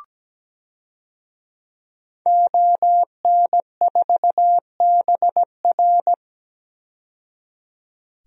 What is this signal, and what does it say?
Morse code, heard as a keyed tone, about 17 wpm, 700 Hz. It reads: ON4BR
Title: DXCC-100 Call Signs - 5 Characters - 17wpm